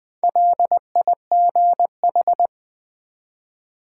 Morse code, heard as a keyed tone, about 20 wpm, 700 Hz. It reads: LIGH
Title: Build A Word - 2 Letters At A Time - 8-Letter Words - 20wpm